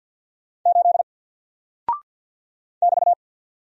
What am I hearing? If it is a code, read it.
Morse code, heard as a keyed tone, about 48 wpm, 700 Hz. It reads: 8 <BT>
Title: Single Character (Letter:Number:Punctuation:Prosign) Alt 50wpm